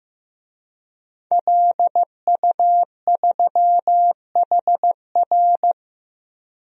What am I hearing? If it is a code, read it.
Morse code, heard as a keyed tone, about 15 wpm, 700 Hz. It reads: LU3HR
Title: DXCC-100 Call Signs - 5 Characters - 15wpm